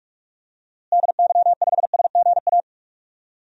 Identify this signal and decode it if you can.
Morse code, heard as a keyed tone, about 46 wpm, 690 Hz. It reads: DY5SGA
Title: DXCC-100 Call Signs - 6 Characters - Repeated 2x - 45wpm